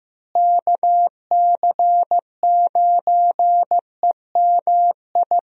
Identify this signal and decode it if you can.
Morse code, heard as a keyed tone, about 15 wpm, 700 Hz. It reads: KC9EMI
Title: USA Call Signs - 6 Characters - Repeated 2x - 15wpm